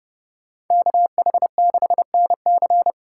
Morse code, heard as a keyed tone, about 28 wpm, 700 Hz. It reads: KH6NC